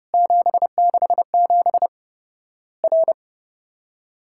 Morse code, heard as a keyed tone, about 30 wpm, 700 Hz. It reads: 767 R
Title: Worked All Europe DX Contest - QTC Exchange - 30wpm